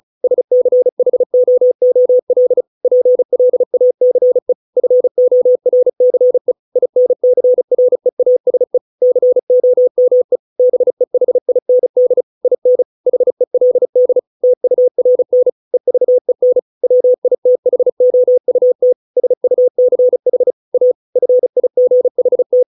Morse code, heard as a keyed tone, about 34 wpm, 500 Hz. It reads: SCHOOL PLACE FORCE INCREASE COME BEHIND IN HELD TURN EVEN WITHOUT SUCH A FIGHT